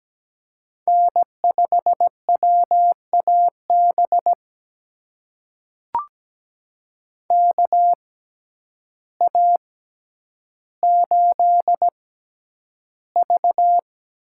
Morse code, heard as a keyed tone, about 17 wpm, spaced as for 9 wpm, 700 Hz. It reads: N5WAB KA8V
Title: USA Call Signs - 6x Character Spacing - 17wpm